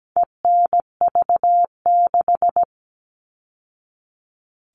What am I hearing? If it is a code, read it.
Morse code, heard as a keyed tone, about 17 wpm, 700 Hz. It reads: ENV6
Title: All Cut Numbers - 4-Digit 17wpm